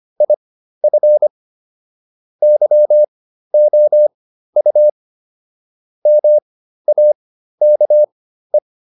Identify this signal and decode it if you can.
Morse code, heard as a keyed tone, about 25 wpm, spaced as for 13 wpm, 600 Hz. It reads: IF YOU MAKE